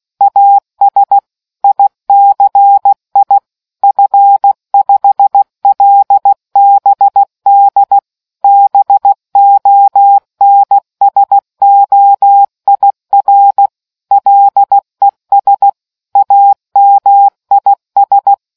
Morse code, heard as a keyed tone, about 16 wpm, 800 Hz. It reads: AS ICI F5LBD BONSOIR LES AMIS